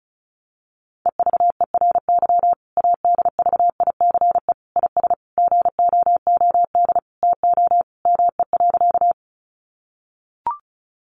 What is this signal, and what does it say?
Morse code, heard as a keyed tone, about 34 wpm, 700 Hz. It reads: EVERY ADVICE IS GOOD TO ME.
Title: Sentences from Top 1100 Words 35wpm